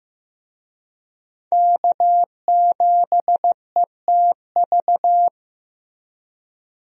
Morse code, heard as a keyed tone, about 15 wpm, 700 Hz. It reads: K7ETV
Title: USA Call Signs - 5 Characters - Repeated 3x - 15wpm